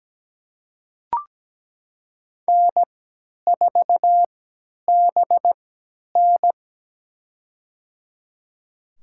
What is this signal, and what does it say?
Morse code, heard as a keyed tone, about 17 wpm, spaced as for 10 wpm, 700 Hz. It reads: N4BN
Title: USA Call Signs - 4 Characters - 3x Character Spacing - 17wpm